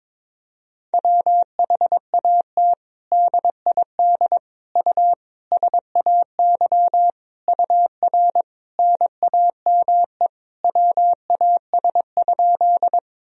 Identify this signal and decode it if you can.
Morse code, heard as a keyed tone, about 22 wpm, 700 Hz. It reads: WHAT DID U SAY UR NAME WAS?
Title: Sentences from Top 200 Words and Abbreviations 22wpm